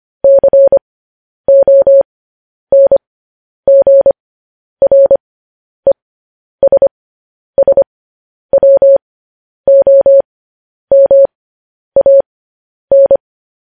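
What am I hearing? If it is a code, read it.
Morse code, heard as a keyed tone, about 25 wpm, spaced as for 10 wpm, 550 Hz. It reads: CONGRESSWOMAN
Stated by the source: Morse news headlines at 10.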